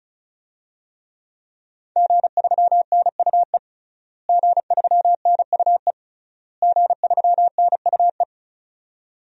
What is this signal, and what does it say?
Morse code, heard as a keyed tone, about 34 wpm, 700 Hz. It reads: G3NUE G3NUE G3NUE
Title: DXCC-100 Call Signs - 5 Characters - Repeated 3x - 35wpm